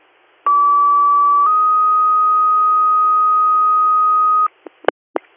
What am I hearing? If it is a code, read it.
Two-tone page, heard as a one-second tone, then a three-second tone, about 1150 then 1250 Hz. It is heard from 0.5 to 4.5 s.